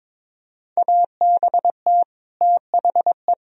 Morse code, heard as a keyed tone, about 22 wpm, 700 Hz. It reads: ABT THE
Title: Sentences from Top 400 Words and Abbreviations 22wpm